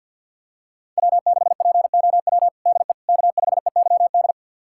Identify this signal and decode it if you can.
Morse code, heard as a keyed tone, about 47 wpm, 700 Hz. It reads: W6POW DE K5EYD
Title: USA Callsign DE Callsign 50wpm